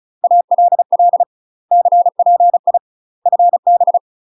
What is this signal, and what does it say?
Morse code, heard as a keyed tone, about 36 wpm, 700 Hz. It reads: ALL CPI FB
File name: UK MASTER QSOs TRAINING / 35WPM / UK MASTER QSOs 0033